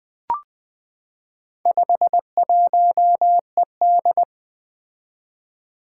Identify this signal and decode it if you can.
Morse code, heard as a keyed tone, about 20 wpm, 700 Hz. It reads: 51ED
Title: Four Letter-Number Combinations - 20wpm